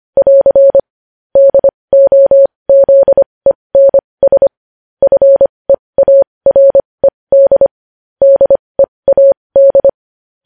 Morse code, heard as a keyed tone, about 25 wpm, spaced as for 20 wpm, 550 Hz. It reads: <AR> DOZENS FEARED DEAD
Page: News Headlines in Morse Code at 20 WPM